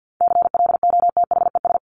Morse code, heard as a keyed tone, about 50 wpm, 700 Hz. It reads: CLOTHES
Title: Top 1751-2000 Words 50wpm